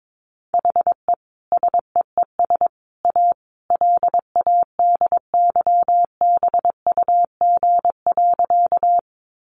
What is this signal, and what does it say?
Morse code, heard as a keyed tone, about 22 wpm, 700 Hz. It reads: HE SEES A LADYBUG.
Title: Additional Easy Phrases 22wpm